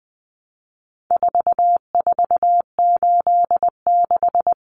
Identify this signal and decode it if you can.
Morse code, heard as a keyed tone, about 20 wpm, 700 Hz. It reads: 4486
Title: Numbers - 4-Digit 20wpm